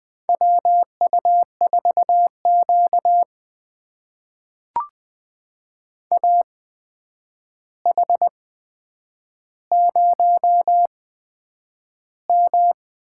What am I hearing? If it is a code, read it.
Morse code, heard as a keyed tone, about 20 wpm, spaced as for 9 wpm, 700 Hz. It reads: WU4Q AH0M
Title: USA Call Signs - 4 Characters - 8x Character Spacing - 20wpm